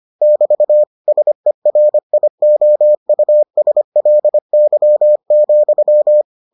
Morse code, heard as a keyed tone, about 25 wpm, 600 Hz. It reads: <BT> SERIOUSLY,